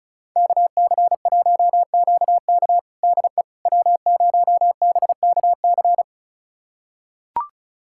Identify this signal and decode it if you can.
Morse code, heard as a keyed tone, about 34 wpm, 690 Hz. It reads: KC1QK DE W0BKC
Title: USA Callsign DE Callsign - 5 Characters - 35wpm